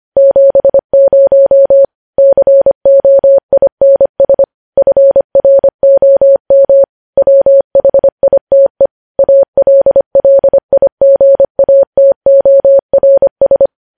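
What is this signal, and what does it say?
Morse code, heard as a keyed tone, about 25 wpm, 550 Hz. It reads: 70 COINS FROM WHITE ALLIGATORS
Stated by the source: Morse practice sending news headlines at 25 wpm.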